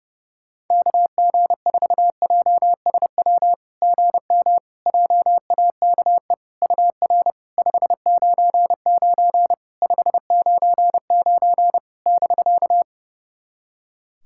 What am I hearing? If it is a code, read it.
Morse code, heard as a keyed tone, about 32 wpm, 700 Hz. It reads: KG4JSW GM JAKE UR 599 599 <BK>
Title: SOTA QSO Practice 30wpm